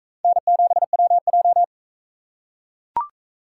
Morse code, heard as a keyed tone, about 42 wpm, 700 Hz. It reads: N7WJ